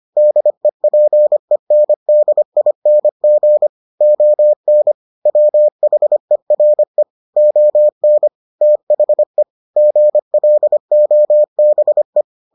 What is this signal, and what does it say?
Morse code, heard as a keyed tone, about 25 wpm, 600 Hz. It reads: DEPENDING ON WHERE ON THE GLOBE